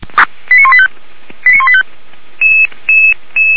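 Five-tone selective call (ZVEI), heard as quick runs of five tones, tones of about 70 ms. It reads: 89185 89185
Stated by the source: ZVEI five-tone selective call, code 89132, received by SDR; recording hosted by feuerwehr-orlamuende.org (fire brigade Orlamünde).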